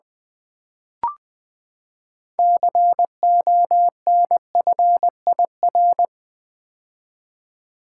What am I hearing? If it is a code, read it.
Morse code, heard as a keyed tone, about 20 wpm, 700 Hz. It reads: CONFIR
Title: Build A Word - 6 Letters At A Time - 12-Letter Words - 20wpm